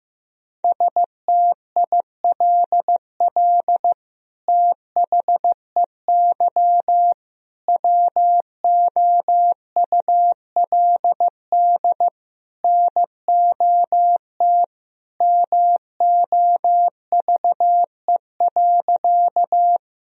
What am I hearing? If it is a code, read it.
Morse code, heard as a keyed tone, about 15 wpm, 700 Hz. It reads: STILL THEY WOULD NOT MOVE.
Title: Sentences from Top 1000 Words - Review Part 1 - 15wpm